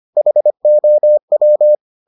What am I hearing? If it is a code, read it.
Morse code, heard as a keyed tone, about 25 wpm, 600 Hz. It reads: HOW